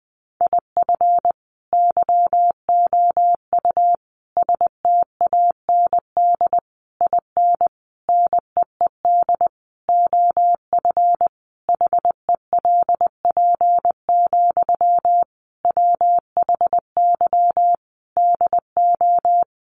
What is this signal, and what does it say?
Morse code, heard as a keyed tone, about 20 wpm, 700 Hz. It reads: IF YOU STAND IN NEED OF HELP, WHY DO
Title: Sentences from Top 1200 Words 20wpm